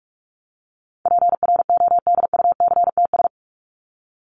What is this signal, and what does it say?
Morse code, heard as a keyed tone, about 44 wpm, 700 Hz. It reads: PRODUCTS
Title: ES:S Suffix - Top 100 Words with Base Words 45wpm